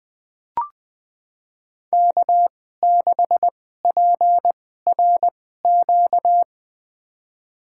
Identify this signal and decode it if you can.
Morse code, heard as a keyed tone, about 21 wpm, spaced as for 15 wpm, 700 Hz. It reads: K6PRQ
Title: USA Call Signs - 5 Characters - 2x Character Spacing - 20wpm